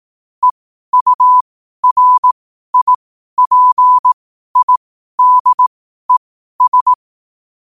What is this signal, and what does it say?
Morse code, heard as a keyed tone, about 18 wpm, spaced as for 12 wpm, 1000 Hz. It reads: EURIPIDES
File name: qotdfiles / 12wpm